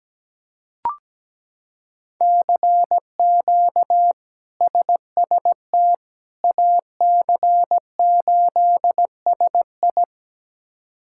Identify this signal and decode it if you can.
Morse code, heard as a keyed tone, about 17 wpm, 700 Hz. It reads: CQ SST AC8SI
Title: K1USN SST QSO Practice - 17wpm